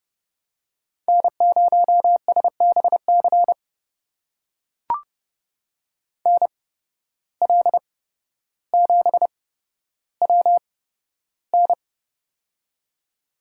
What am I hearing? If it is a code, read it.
Morse code, heard as a keyed tone, about 30 wpm, spaced as for 13 wpm, 700 Hz. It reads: N0SBC NL7WN
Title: USA Call Signs - 5 Characters - 8x Character Spacing - 30wpm